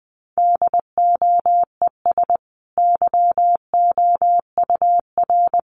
Morse code, heard as a keyed tone, about 20 wpm, 700 Hz. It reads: DOES YOUR
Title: Sentences from Top 1400 Words 20wpm